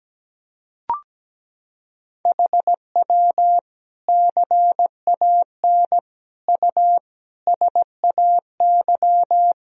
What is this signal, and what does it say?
Morse code, heard as a keyed tone, about 17 wpm, 700 Hz. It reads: HW CAN U SAY
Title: Sentences from Top 200 Words and Abbreviations 17wpm